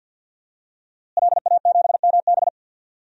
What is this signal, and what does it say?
Morse code, heard as a keyed tone, about 51 wpm, 700 Hz. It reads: LA7MB
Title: DXCC-100 Call Signs - 5 Characters - Repeated 2x - 50wpm